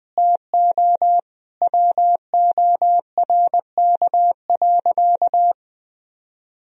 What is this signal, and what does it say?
Morse code, heard as a keyed tone, about 21 wpm, 700 Hz. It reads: TO WORK.
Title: Sentences from Top 1000 Words - Review Part 1 - 20wpm